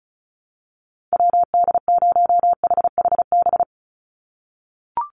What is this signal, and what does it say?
Morse code, heard as a keyed tone, about 34 wpm, 700 Hz. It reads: WD0HHB
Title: USA Call Signs 35wpm